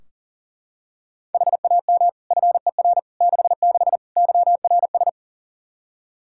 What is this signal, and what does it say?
Morse code, heard as a keyed tone, about 40 wpm, 700 Hz. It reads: HAM FER 66 YRS
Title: K1USN SST QSO Practice 40wpm